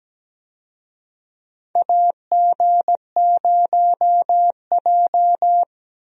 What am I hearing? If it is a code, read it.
Morse code, heard as a keyed tone, about 17 wpm, 700 Hz. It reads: AG0J
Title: USA Call Signs - 4 Characters - Repeated 2x - 17wpm